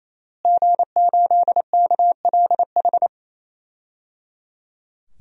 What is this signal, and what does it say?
Morse code, heard as a keyed tone, about 28 wpm, 700 Hz. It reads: G8KLH
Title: DXCC-100 Call Signs - 5 Characters - Repeated 3x - 28wpm